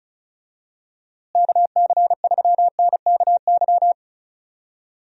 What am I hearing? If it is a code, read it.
Morse code, heard as a keyed tone, about 36 wpm, 700 Hz. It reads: KC3NKY
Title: USA Call Signs - 6 Characters - Repeated 3x - 35wpm